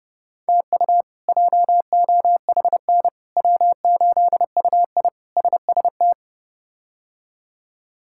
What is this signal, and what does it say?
Morse code, heard as a keyed tone, about 31 wpm, 700 Hz. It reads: TU JOHN W8UI SST
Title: K1USN SST QSO Practice - 30wpm